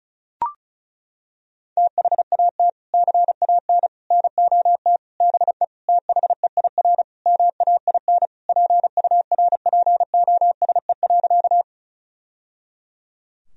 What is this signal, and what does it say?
Morse code, heard as a keyed tone, about 34 wpm, 700 Hz. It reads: THAT CAN NOT BE THEIR MAIN PURPOSE.
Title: Sentences from Top 1000 Words - Review Part 2 - 35wpm